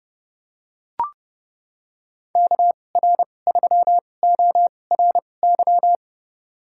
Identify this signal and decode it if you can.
Morse code, heard as a keyed tone, about 30 wpm, spaced as for 22 wpm, 700 Hz. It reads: KR3ORY
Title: USA Call Signs - 6 Characters - 2x Character Spacing - 30wpm